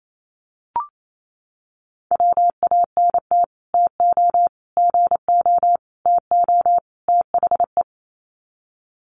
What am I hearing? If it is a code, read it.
Morse code, heard as a keyed tone, about 28 wpm, 700 Hz. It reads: WANT TO GO TO THE
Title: Sets of 5 Words Encore2 28wpm